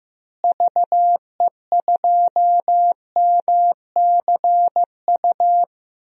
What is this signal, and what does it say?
Morse code, heard as a keyed tone, about 15 wpm, 700 Hz. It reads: VE2MCU
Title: DXCC-100 Call Signs - 6 Characters - Repeated 3x - 15wpm